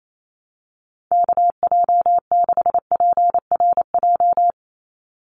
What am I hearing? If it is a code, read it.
Morse code, heard as a keyed tone, about 28 wpm, 700 Hz. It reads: KJ6PRJ
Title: USA Call Signs 28wpm